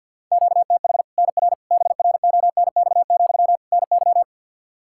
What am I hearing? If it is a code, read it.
Morse code, heard as a keyed tone, about 50 wpm, 700 Hz. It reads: QTH NR BRONX, NY
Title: Practice QSO - Exchange 1 - Beginning - 50wpm